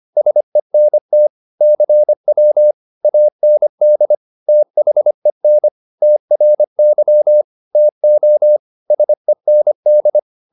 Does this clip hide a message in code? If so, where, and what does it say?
Yes: Morse code from 0.2 to 10.2 s, heard as a keyed tone, about 25 wpm, 600 Hz. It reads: SENT CW AND THEN TRY TO SEND